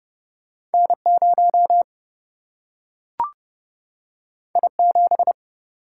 Morse code, heard as a keyed tone, about 30 wpm, 700 Hz. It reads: N 0 I 7
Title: Single-Letter Single-Number 30wpm